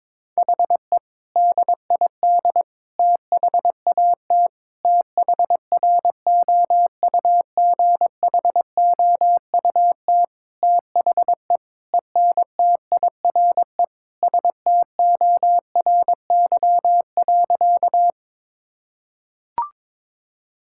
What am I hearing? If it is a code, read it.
Morse code, heard as a keyed tone, about 21 wpm, 700 Hz. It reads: HE DID THAT THROUGHOUT THE ENTIRE STORY.
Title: Sentences from Top 1000 Words - Review Part 5 - 22wpm